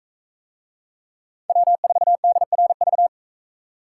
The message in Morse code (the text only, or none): W4DRV